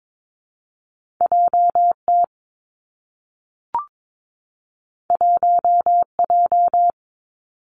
Morse code, heard as a keyed tone, about 22 wpm, 700 Hz. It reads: J T 1 J
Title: Two Challenging Letter-Number Combinations 22wpm